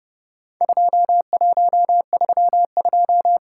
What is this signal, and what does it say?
Morse code, heard as a keyed tone, about 30 wpm, 700 Hz. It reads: 2132
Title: Worked All Europe DX Contest - QTC Exchange - 30wpm